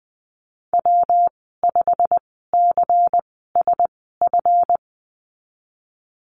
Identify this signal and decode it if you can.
Morse code, heard as a keyed tone, about 19 wpm, spaced as for 14 wpm, 700 Hz. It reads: W5CSF
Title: 167 Lesson 27g - Number 7 - Call Signs 20wpm